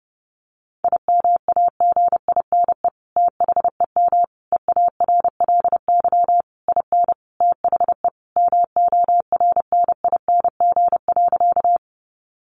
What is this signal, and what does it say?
Morse code, heard as a keyed tone, about 30 wpm, 700 Hz. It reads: IMAGINE THEM EARLY IN THE MORNING.
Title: Sentences from Top 1400 Words 30wpm